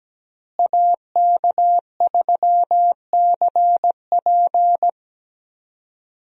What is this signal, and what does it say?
Morse code, heard as a keyed tone, about 17 wpm, 700 Hz. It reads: AK3CP
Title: USA Call Signs - 5 Characters - Repeated 2x - 17wpm